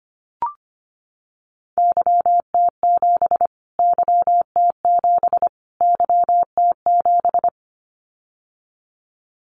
Morse code, heard as a keyed tone, about 25 wpm, 700 Hz. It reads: YT7 YT7 YT7
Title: Callsign Prefixes for Top 25 DXCC - Repeated 3x - 25wpm